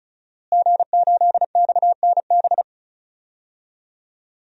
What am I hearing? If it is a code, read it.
Morse code, heard as a keyed tone, about 34 wpm, 700 Hz. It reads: G8XNB
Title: DXCC-100 Call Signs - 5 Characters - Repeated 2x - 35wpm